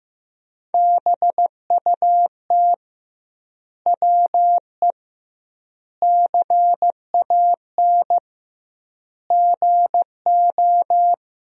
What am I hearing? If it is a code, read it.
Morse code, heard as a keyed tone, about 15 wpm, 700 Hz. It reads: BUT WE CAN GO
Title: 2x Word-Spacing - Sentences from Top 100 Words - 15wpm